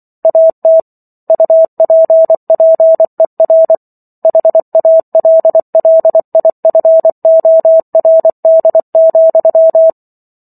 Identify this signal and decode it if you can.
Morse code, heard as a keyed tone, about 24 wpm, 650 Hz. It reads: AT UPPER HALLIFORD,